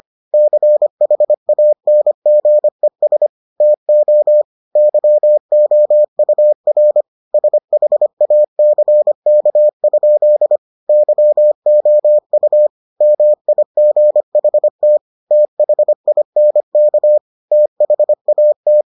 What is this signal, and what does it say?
Morse code, heard as a keyed tone, about 25 wpm, 600 Hz. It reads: CHANGES TO YOUR SHACK? YOU MIGHT THINK THAT